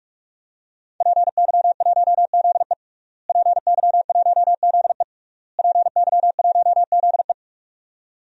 Morse code, heard as a keyed tone, about 45 wpm, 700 Hz. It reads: PY1ZE PY1ZE PY1ZE
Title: DXCC-100 Call Signs - 5 Characters - Repeated 3x - 45wpm